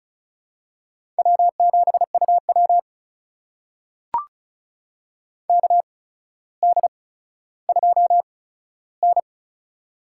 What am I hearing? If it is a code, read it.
Morse code, heard as a keyed tone, about 36 wpm, spaced as for 15 wpm, 700 Hz. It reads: W7UW KD2N
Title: USA Call Signs - 4 Characters - 8x Character Spacing - 35wpm